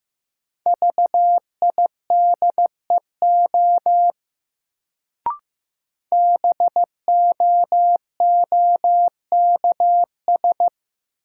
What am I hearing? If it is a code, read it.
Morse code, heard as a keyed tone, about 15 wpm, 700 Hz. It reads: VIDEO BOOKS
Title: Sentences from Top 1000 Words - Review Part 4 - New Word Drill - 15wpm